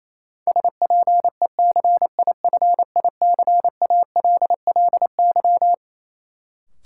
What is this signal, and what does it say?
Morse code, heard as a keyed tone, about 28 wpm, 700 Hz. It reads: SPECIFICALLY
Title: Sentences from Top 1100 Words - New Word Drill - 28wpm